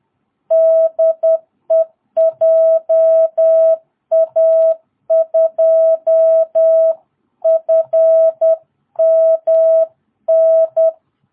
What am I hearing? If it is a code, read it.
Morse code, heard as a keyed tone, about 10 wpm, 660 Hz. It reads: DE JA2FMN
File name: jh8cbh / 2022cw